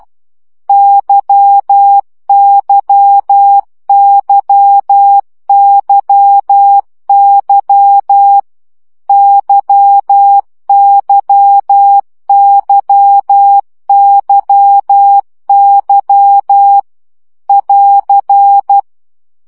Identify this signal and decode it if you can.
Morse code, heard as a keyed tone, about 12 wpm, 800 Hz. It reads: YYYYY YYYYY <AR>
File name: CW gyakorlas y 10 baud